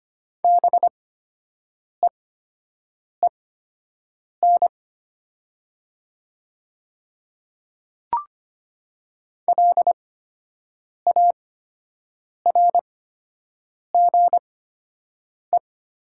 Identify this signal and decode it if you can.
Morse code, heard as a keyed tone, about 25 wpm, spaced as for 7 wpm, 700 Hz. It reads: BEEN LARGE
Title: 8x Character Spacing - Top 200 Words - 25wpm